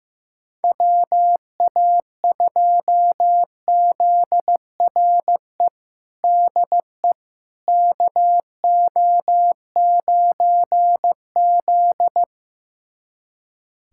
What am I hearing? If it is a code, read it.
Morse code, heard as a keyed tone, about 15 wpm, 700 Hz. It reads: WA2ZRE DE KO9Z
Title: USA Callsign DE Callsign 15wpm